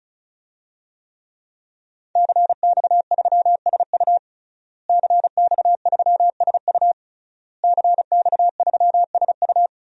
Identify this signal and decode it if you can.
Morse code, heard as a keyed tone, about 34 wpm, 700 Hz. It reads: CX3SU CX3SU CX3SU
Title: DXCC-100 Call Signs - 5 Characters - Repeated 3x - 35wpm